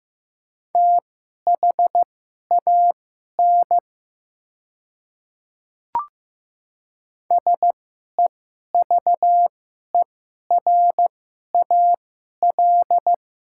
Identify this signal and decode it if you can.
Morse code, heard as a keyed tone, about 15 wpm, spaced as for 11 wpm, 690 Hz. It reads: THAN SEVERAL
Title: 2x Character Spacing - Top 500 Words - 15wpm